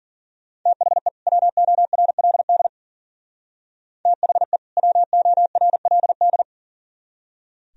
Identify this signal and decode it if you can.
Morse code, heard as a keyed tone, about 49 wpm, 700 Hz. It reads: THE WORLD THE WORLD
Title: Speed-Racing - Sets of 2 Words 40wpm